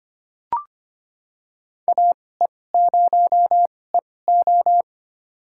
Morse code, heard as a keyed tone, about 25 wpm, spaced as for 18 wpm, 700 Hz. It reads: AE0EO
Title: USA Call Signs - 5 Characters - 2x Character Spacing - 25wpm